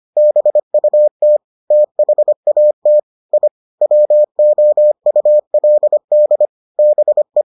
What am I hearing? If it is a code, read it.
Morse code, heard as a keyed tone, about 25 wpm, 600 Hz. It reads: BUT THAT I WOULD BE